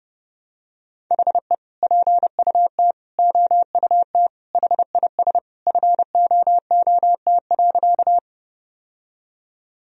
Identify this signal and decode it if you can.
Morse code, heard as a keyed tone, about 29 wpm, 700 Hz. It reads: HE PUT OUT HIS FOOT.
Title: Sentences from Top 1000 Words - Review Part 3 - 30wpm